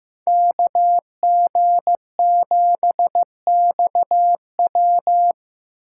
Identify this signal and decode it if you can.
Morse code, heard as a keyed tone, about 15 wpm, 700 Hz. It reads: KG7XW
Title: USA Call Signs - 5 Characters - 2x Character Spacing - 15wpm